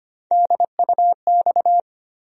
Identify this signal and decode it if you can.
Morse code, heard as a keyed tone, about 25 wpm, 700 Hz. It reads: DUX
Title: USA Call Sign Postfixes - 3 Letters 25wpm